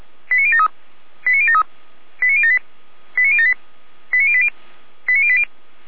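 Five-tone selective call (ZVEI), heard as quick runs of five tones, tones of about 70 ms. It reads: 89073 89073 89078 89078 89080 89080
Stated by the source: ZVEI five-tone call 89067 received by SDR, archived by Feuerwehr Orlamünde (fire brigade).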